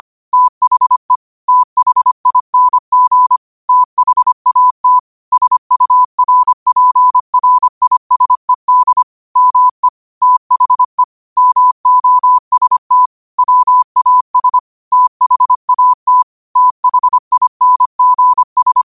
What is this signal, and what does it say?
Morse code, heard as a keyed tone, about 25 wpm, 1000 Hz. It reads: THE THING THAT SURPRISED ME THE MOST WAS THAT THINGS
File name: qotdfiles / 25wpm